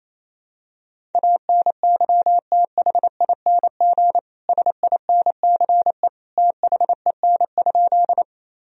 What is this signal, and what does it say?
Morse code, heard as a keyed tone, about 28 wpm, 700 Hz. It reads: ANYTHING SINCE THEN?
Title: Sentences from Top 1000 Words - Review Part 1 - 28wpm